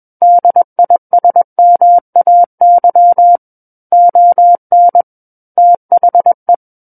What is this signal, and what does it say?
Morse code, heard as a keyed tone, about 21 wpm, 700 Hz. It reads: DISMAY ON THE
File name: Book One / 12-Chapter 12 - 21 WPM